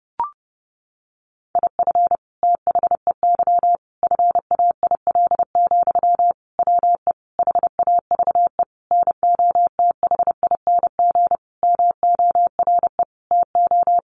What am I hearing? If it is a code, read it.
Morse code, heard as a keyed tone, about 30 wpm, 700 Hz. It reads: IF THEY FAIL, WE HAVE NOTHING MORE TO
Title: Sentences from Top 1100 Words 30wpm